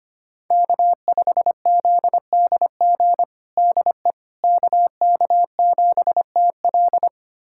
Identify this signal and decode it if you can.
Morse code, heard as a keyed tone, about 25 wpm, 700 Hz. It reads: K5ZDG DE KK7TL
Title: USA Callsign DE Callsign - 5 Characters - 25wpm